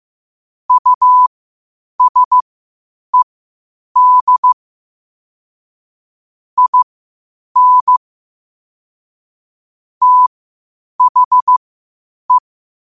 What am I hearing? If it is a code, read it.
Morse code, heard as a keyed tone, about 15 wpm, spaced as for 8 wpm, 1000 Hz. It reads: USED IN THE
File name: qotdfiles / 8wpm